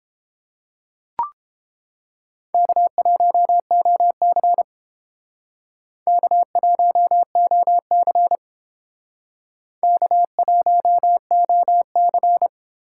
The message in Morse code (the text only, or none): K1OC K1OC K1OC